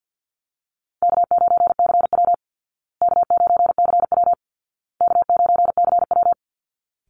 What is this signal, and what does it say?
Morse code, heard as a keyed tone, about 50 wpm, 700 Hz. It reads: K9CW K9CW K9CW
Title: Call Signs Repeated 2x - 50wpm